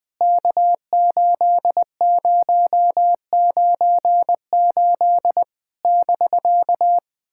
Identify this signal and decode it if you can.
Morse code, heard as a keyed tone, about 21 wpm, 700 Hz. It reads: K8098 <BK>
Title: POTA QSO Practice - 20wpm